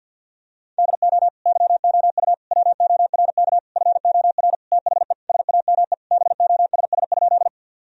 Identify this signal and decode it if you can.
Morse code, heard as a keyed tone, about 50 wpm, 700 Hz. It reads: DO YOU WORK FOR THE SAME BOSS?
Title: Sentences from Top 1000 Words - Review Part 4 - 50wpm